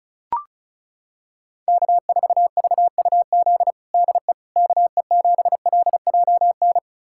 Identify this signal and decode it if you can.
Morse code, heard as a keyed tone, about 36 wpm, 700 Hz. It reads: K4VUZ DE KE7LJN
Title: USA Callsign DE Callsign 35wpm